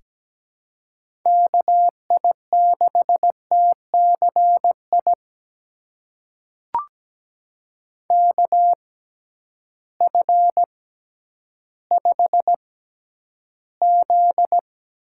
Morse code, heard as a keyed tone, about 17 wpm, spaced as for 10 wpm, 700 Hz. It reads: KI6TCI KF5Z
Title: USA Call Signs - 6 Characters - 6x Character Spacing - 17wpm